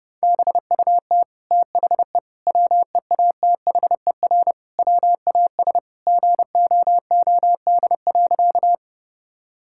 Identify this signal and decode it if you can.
Morse code, heard as a keyed tone, about 30 wpm, 700 Hz. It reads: BUT THE WEATHER WAS GOOD.
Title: Sentences from Top 1000 Words - Review Part 5 - 30wpm